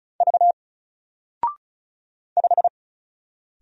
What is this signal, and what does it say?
Morse code, heard as a keyed tone, about 34 wpm, 700 Hz. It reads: V 5
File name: Single-Letter-Number-35wpm